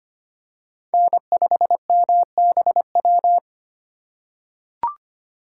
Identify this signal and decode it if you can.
Morse code, heard as a keyed tone, about 25 wpm, 700 Hz. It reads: N5MBW
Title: USA Call Signs - 5 Characters - 2x Character Spacing - 25wpm